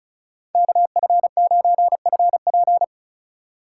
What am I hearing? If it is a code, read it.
Morse code, heard as a keyed tone, about 36 wpm, 700 Hz. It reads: KF9FP
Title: USA Call Signs - 5 Characters - 3x Character Spacing - 35wpm